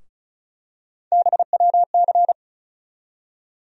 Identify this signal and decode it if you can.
Morse code, heard as a keyed tone, about 36 wpm, 700 Hz. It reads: BWC
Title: USA Call Sign Postfixes - 3 Letters 35wpm